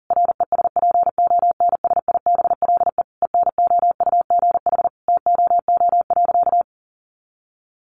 Morse code, heard as a keyed tone, about 40 wpm, 700 Hz. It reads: RESPONSIBLE ENOUGH TOO.
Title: Additional Easy Phrases 40wpm